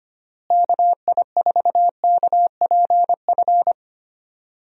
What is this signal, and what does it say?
Morse code, heard as a keyed tone, about 25 wpm, 700 Hz. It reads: KI4KPF
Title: USA Call Signs - 6 Characters - Repeated 3x - 25wpm